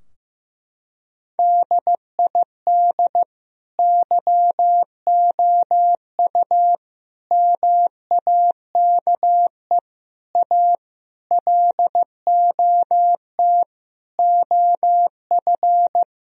Morse code, heard as a keyed tone, about 15 wpm, 700 Hz. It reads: DID YOU MAKE A LOT OF
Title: Sentences from Top 1000 Words - Review Part 4 - 15wpm